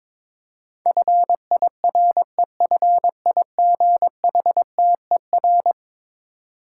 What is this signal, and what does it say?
Morse code, heard as a keyed tone, about 22 wpm, 700 Hz. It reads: FIREFIGHTER
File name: practice / 22wpm / Eleven Letter Words - 22wpm